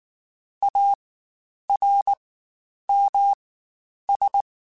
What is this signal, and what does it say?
Morse code, heard as a keyed tone, about 19 wpm, spaced as for 9 wpm, 800 Hz. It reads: ARMS
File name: Quotes Spaced / Quotes Spaced-mp3s / ChildAsleep sp @18